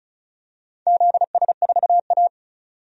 Morse code, heard as a keyed tone, about 34 wpm, 700 Hz. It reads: ZS4A
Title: DXCC-100 Call Signs - 4 Characters - 35wpm